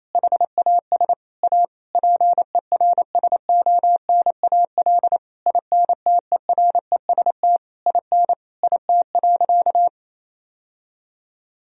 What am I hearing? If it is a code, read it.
Morse code, heard as a keyed tone, about 28 wpm, 700 Hz. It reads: HAS A PERSONAL INTEREST IN IT.
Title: Sentences from Top 1000 Words - Review Part 4 - 28wpm